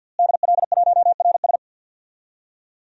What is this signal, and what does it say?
Morse code, heard as a keyed tone, about 51 wpm, 700 Hz. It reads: DL1RS
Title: DXCC-100 Call Signs - 5 Characters - Repeated 3x - 50wpm